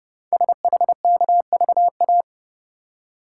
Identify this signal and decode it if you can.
Morse code, heard as a keyed tone, about 30 wpm, 700 Hz. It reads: SHKVA
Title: Code Groups - 5 Letters - 30wpm